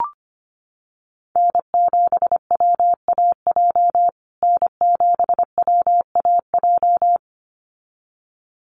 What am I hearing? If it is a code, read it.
Morse code, heard as a keyed tone, about 25 wpm, 700 Hz. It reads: N7WAJ N7WAJ
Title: USA Call Signs Repeated 2x 25wpm